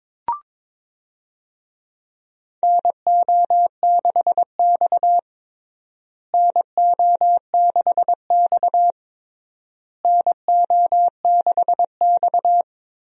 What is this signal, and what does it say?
Morse code, heard as a keyed tone, about 22 wpm, 700 Hz. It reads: NO6X NO6X NO6X
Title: USA Call Signs - 4 Characters - Repeated 3x - 22wpm